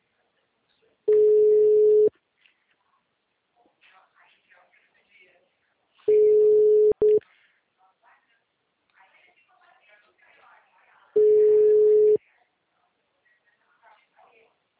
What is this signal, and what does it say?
Ringback tone (Europe, 425 Hz), with speech.
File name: out- -2006- - - .91368